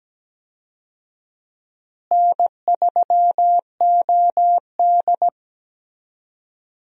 Morse code, heard as a keyed tone, about 17 wpm, 700 Hz. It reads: N3OD
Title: USA Call Signs - 4 Characters - Repeated 2x - 17wpm